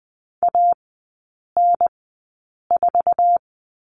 Morse code, heard as a keyed tone, about 20 wpm, spaced as for 8 wpm, 700 Hz. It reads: AN4
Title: 114 Lesson 19e - Letter w - Three Characters 20wpm